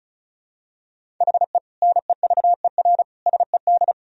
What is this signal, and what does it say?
Morse code, heard as a keyed tone, about 36 wpm, 700 Hz. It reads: HE NEVER SED
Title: Sentences from Top 500 Words and Abbreviations 35wpm